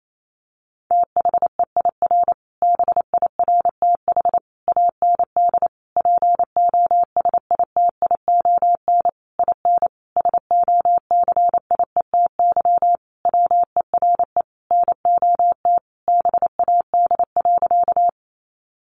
Morse code, heard as a keyed tone, about 28 wpm, 700 Hz. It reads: THEIR BIRTH AND POSITION IN SOCIETY WERE NOT BAD.
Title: Sentences from Top 900 Words 28wpm